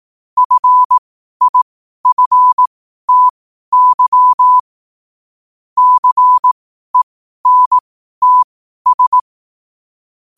Morse code, heard as a keyed tone, about 18 wpm, spaced as for 12 wpm, 1000 Hz. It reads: FIFTY CENTS
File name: qotdfiles / 12wpm